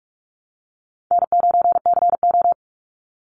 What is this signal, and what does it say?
Morse code, heard as a keyed tone, about 44 wpm, 700 Hz. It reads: N9CO
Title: Easy Call Signs Repeated 2x - 45wpm